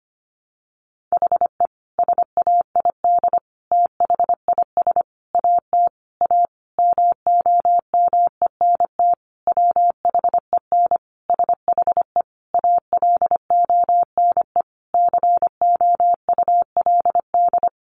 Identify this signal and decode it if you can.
Morse code, heard as a keyed tone, about 25 wpm, 700 Hz. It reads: HE SAID THIS AT A MOMENT WHEN SHE ALONE COULD
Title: Sentences from Top 1000 Words 25wpm